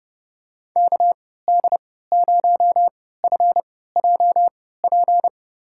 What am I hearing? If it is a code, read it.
Morse code, heard as a keyed tone, about 32 wpm, spaced as for 17 wpm, 700 Hz. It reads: KD0FJP
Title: USA Call Signs - 6 Characters - 3x Character Spacing - 30wpm